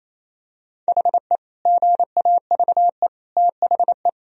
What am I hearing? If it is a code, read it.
Morse code, heard as a keyed tone, about 28 wpm, 700 Hz. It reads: HE GAVE THE
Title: Sentences from Top 1000 Words - Review Part 2 - 28wpm